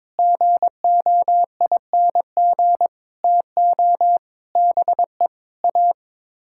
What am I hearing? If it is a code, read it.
Morse code, heard as a keyed tone, about 22 wpm, 700 Hz. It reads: GOING TO BE A
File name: practice / 22wpm / Sets of 7 Words - 22wpm